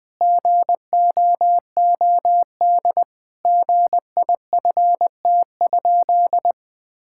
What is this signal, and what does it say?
Morse code, heard as a keyed tone, about 21 wpm, 700 Hz. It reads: GOOD GIFT?
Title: Sentences from Top 1000 Words - Review Part 6 - 20wpm